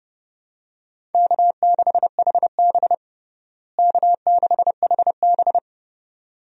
Morse code, heard as a keyed tone, about 30 wpm, 700 Hz. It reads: K6HB K6HB
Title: USA Call Signs - 4 Characters - Repeated 2x - 30wpm